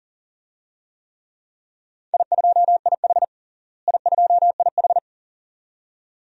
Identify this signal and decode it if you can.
Morse code, heard as a keyed tone, about 40 wpm, 700 Hz. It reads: I2IH I2IH
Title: DXCC-100 Call Signs - 4 Characters - Repeated 2x - 40wpm